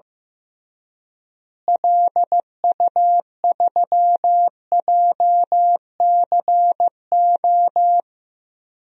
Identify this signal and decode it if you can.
Morse code, heard as a keyed tone, about 15 wpm, 700 Hz. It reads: LU3JCO
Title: DXCC-100 Call Signs - 6 Characters - Repeated 3x - 15wpm